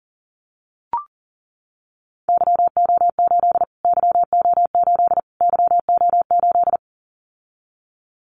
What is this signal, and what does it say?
Morse code, heard as a keyed tone, about 40 wpm, 700 Hz. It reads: YO8 YO8 YO8
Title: Callsign Prefixes for Top 25 DXCC - Repeated 3x - 40wpm